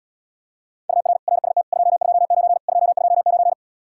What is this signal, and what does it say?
Morse code, heard as a keyed tone, about 80 wpm, 700 Hz. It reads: UR RST 229 229